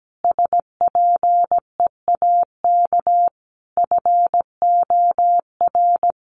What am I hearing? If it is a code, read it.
Morse code, heard as a keyed tone, about 17 wpm, 700 Hz. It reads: SPEAK FOR
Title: Sentences from Top 1400 Words 17wpm